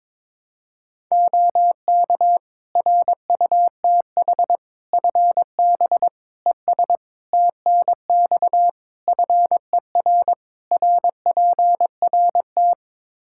Morse code, heard as a keyed tone, about 21 wpm, 700 Hz. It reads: OK RUTH FB ES TNX FER RPRT
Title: Practice QSO - Exchange 2 - Info - 22wpm